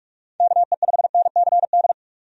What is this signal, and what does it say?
Morse code, heard as a keyed tone, about 42 wpm, 700 Hz. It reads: KE5NCD
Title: USA Call Signs - 6 Characters - 45wpm